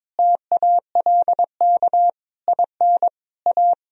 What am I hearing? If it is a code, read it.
Morse code, heard as a keyed tone, about 22 wpm, 700 Hz. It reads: TALK IN A
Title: Sentences from Top 1000 Words - Review Part 1 - 22wpm